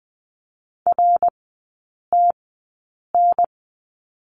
Morse code, heard as a keyed tone, about 20 wpm, spaced as for 8 wpm, 700 Hz. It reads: RTN
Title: 053 Lesson 10e - Letter r - Three Characters 20wpm